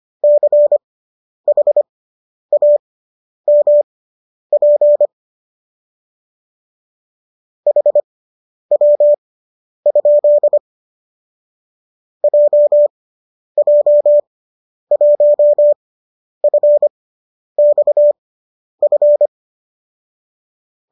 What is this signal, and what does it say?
Morse code, heard as a keyed tone, about 25 wpm, spaced as for 10 wpm, 590 Hz. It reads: CHAMP HW? JJ1FXF
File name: qso005 10